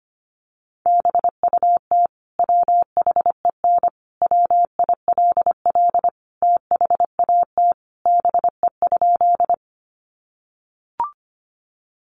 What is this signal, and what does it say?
Morse code, heard as a keyed tone, about 25 wpm, 700 Hz. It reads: BUT WHEN WILL THAT BE?
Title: Sentences from Top 100 Words 25wpm